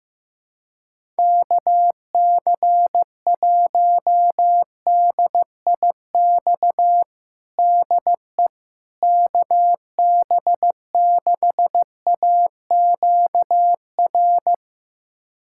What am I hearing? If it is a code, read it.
Morse code, heard as a keyed tone, about 15 wpm, 700 Hz. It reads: KC1DIX DE KB6AQR
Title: USA Callsign DE Callsign 15wpm